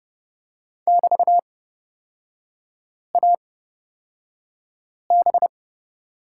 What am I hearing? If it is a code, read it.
Morse code, heard as a keyed tone, about 31 wpm, spaced as for 5 wpm, 700 Hz. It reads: <BT>AB
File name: Lesson 36e - BT - Three Characters 30wpm